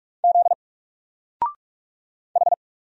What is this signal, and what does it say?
Morse code, heard as a keyed tone, about 46 wpm, 700 Hz. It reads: Z H
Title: Single Character (Letter:Number:Punctuation:Prosign) 45wpm